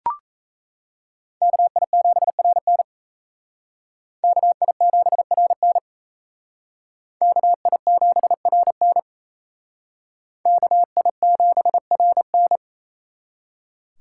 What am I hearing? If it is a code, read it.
Morse code, heard as a keyed tone, about 35 wpm, 700 Hz. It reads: KI7RN KI7RN KI7RN KI7RN
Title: Speed Racing - USA Call Signs 28wpm